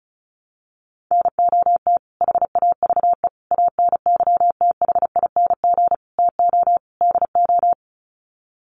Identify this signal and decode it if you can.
Morse code, heard as a keyed tone, about 34 wpm, 700 Hz. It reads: NOT HAVE ANYTHING TO DO
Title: Sets of 5 Words Encore 35wpm